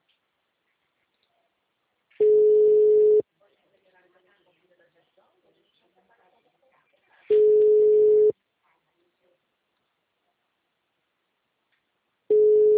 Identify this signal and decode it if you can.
Ringback tone (Europe, 425 Hz), with speech.